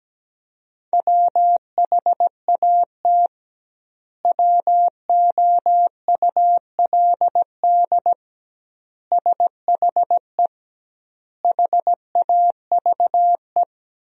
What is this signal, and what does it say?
Morse code, heard as a keyed tone, about 17 wpm, 700 Hz. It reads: WHAT WOULD SHE HAVE
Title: 2x Word-Spacing - Sentences from Top 100 Words - 17wpm